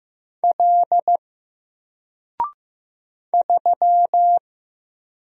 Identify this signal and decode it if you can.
Morse code, heard as a keyed tone, about 15 wpm, 700 Hz. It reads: L 3
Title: Single Character (Letter:Number:Punctuation:Prosign) 15wpm